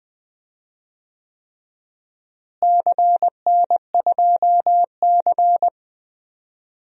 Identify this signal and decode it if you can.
Morse code, heard as a keyed tone, about 19 wpm, 700 Hz. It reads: CN2C
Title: DXCC-100 Call Signs - 4 Characters - Repeated 3x - 20wpm